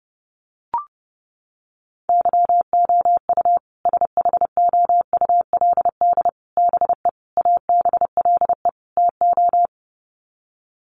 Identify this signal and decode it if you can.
Morse code, heard as a keyed tone, about 30 wpm, 700 Hz. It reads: YOU SHOULD BE ABLE TO
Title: Sets of 5 Words Encore 30wpm